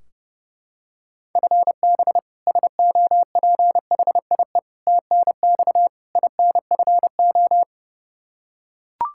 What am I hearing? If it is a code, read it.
Morse code, heard as a keyed tone, about 31 wpm, 700 Hz. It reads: FB SOPHIE TNX INFO
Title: Practice QSO - Exchange 3 - More Info - 30wpm